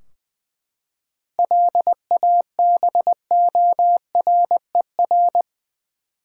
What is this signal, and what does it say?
Morse code, heard as a keyed tone, about 19 wpm, 700 Hz. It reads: LABORER
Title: K1USN SST QSO Practice 20wpm